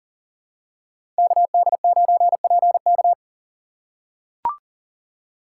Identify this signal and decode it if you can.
Morse code, heard as a keyed tone, about 40 wpm, 700 Hz. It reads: KD9PK
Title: USA Call Signs - 5 Characters - 4x Character Spacing - 40wpm